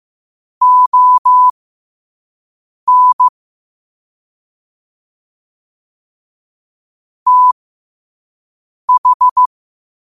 Morse code, heard as a keyed tone, about 15 wpm, spaced as for 5 wpm, 1000 Hz. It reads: ON TH
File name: qotdfiles / 5wpm